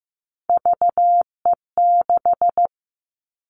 Morse code, heard as a keyed tone, about 15 wpm, 700 Hz. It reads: VE6
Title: Callsign Prefixes for Top 25 DXCC - 15wpm